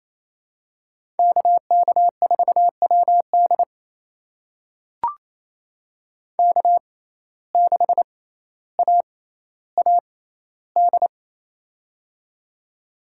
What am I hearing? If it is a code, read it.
Morse code, heard as a keyed tone, about 28 wpm, spaced as for 14 wpm, 700 Hz. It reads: KK4WD K6AAD
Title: USA Call Signs - 5 Characters - 6x Character Spacing - 28wpm